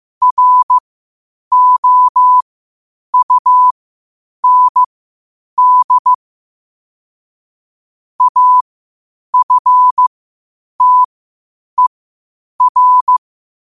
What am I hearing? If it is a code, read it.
Morse code, heard as a keyed tone, about 15 wpm, spaced as for 8 wpm, 1000 Hz. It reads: ROUND AFTER
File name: qotdfiles / 8wpm